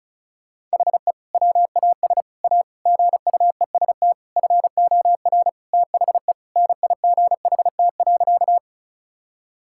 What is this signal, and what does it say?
Morse code, heard as a keyed tone, about 34 wpm, 700 Hz. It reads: HE WAS A GUEST FOR THE NIGHT.
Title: Sentences from Top 1600 Words 35wpm